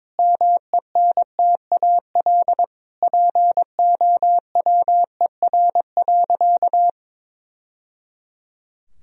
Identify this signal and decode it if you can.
Morse code, heard as a keyed tone, about 22 wpm, 700 Hz. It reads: MENTAL POWER.
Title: Sentences from Top 1000 Words - Review Part 4 - 22wpm